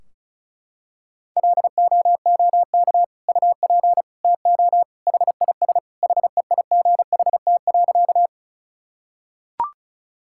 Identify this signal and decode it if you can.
Morse code, heard as a keyed tone, about 36 wpm, 700 Hz. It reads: LOOK UP TO HIS HEIGHT.
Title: Sentences from Top 1600 Words 35wpm